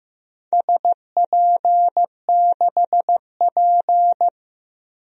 Morse code, heard as a keyed tone, about 15 wpm, 700 Hz. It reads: SP6P
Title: DXCC-100 Call Signs - 4 Characters - Repeated 2x - 15wpm